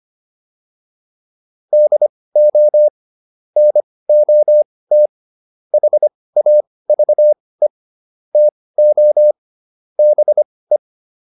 Morse code, heard as a keyed tone, about 25 wpm, spaced as for 18 wpm, 600 Hz. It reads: DO NOT HAVE TO BE